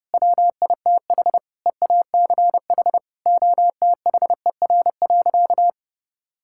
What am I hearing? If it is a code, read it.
Morse code, heard as a keyed tone, about 30 wpm, 700 Hz. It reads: WITH EACH OTHER.
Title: Sentences from Top 1000 Words - Review Part 6 - 30wpm